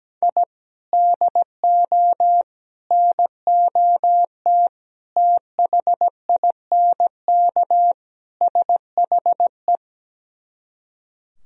Morse code, heard as a keyed tone, about 17 wpm, 700 Hz. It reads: I DO NOT THINK SHE
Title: Speed-Racing - Sets of 5 Words 17wpm